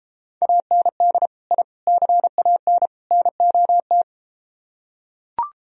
Morse code, heard as a keyed tone, about 33 wpm, 700 Hz. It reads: AND I CAN NOT